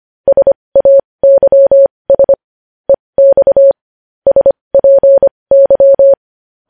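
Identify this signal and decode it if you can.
Morse code, heard as a keyed tone, about 25 wpm, spaced as for 20 wpm, 550 Hz. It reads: SAYS EX SPY
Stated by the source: Morse news headlines practice at 20 wpm.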